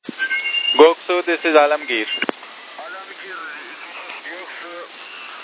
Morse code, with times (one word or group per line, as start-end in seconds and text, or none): none